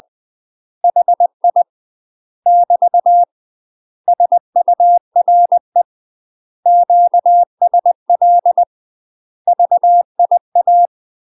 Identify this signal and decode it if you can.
Morse code, heard as a keyed tone, about 20 wpm, 700 Hz. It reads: HI <BT> SURE QSL VIA
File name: UK MASTER QSOs TRAINING / 20WPM / UK MASTER QSOs 0034